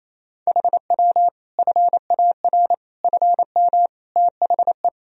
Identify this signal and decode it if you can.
Morse code, heard as a keyed tone, about 28 wpm, 700 Hz. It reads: HW FAR FM THE
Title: Sentences from Top 400 Words and Abbreviations 28wpm